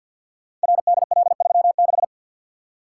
Morse code, heard as a keyed tone, about 47 wpm, 700 Hz. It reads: RDL36
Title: Code Groups - 5 Letter:Numbers - 50wpm